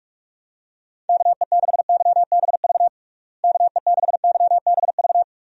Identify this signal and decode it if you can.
Morse code, heard as a keyed tone, about 46 wpm, 700 Hz. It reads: KE6YBV KE6YBV
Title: USA Call Signs - 6 Characters - Repeated 3x - 45wpm